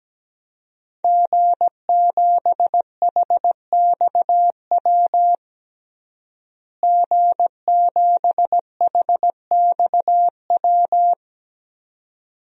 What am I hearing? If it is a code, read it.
Morse code, heard as a keyed tone, about 17 wpm, 700 Hz. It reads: G7HXW G7HXW
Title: DXCC-100 Call Signs - 5 Characters - Repeated 2x - 17wpm